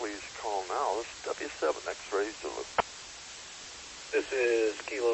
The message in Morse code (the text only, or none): none